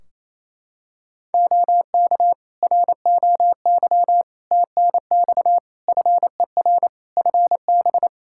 Morse code, heard as a keyed tone, about 27 wpm, 700 Hz. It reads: OK ROY TNX FER FB
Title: Practice QSO - Exchange 4 - Ending - 28wpm